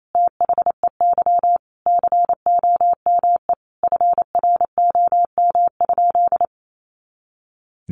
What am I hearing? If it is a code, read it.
Morse code, heard as a keyed tone, about 28 wpm, 700 Hz. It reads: THEY COME FROM?
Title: Sentences from Top 200 Words 28wpm